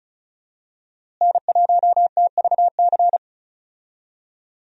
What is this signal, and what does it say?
Morse code, heard as a keyed tone, about 36 wpm, 700 Hz. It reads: N1TVC
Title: USA Call Signs - 5 Characters - Repeated 2x - 35wpm